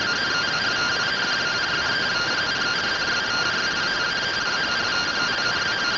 Two-tone teleprinter signal, about 170 Hz shift.
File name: findings / CT AXIAL -111.gen-35 rtty rx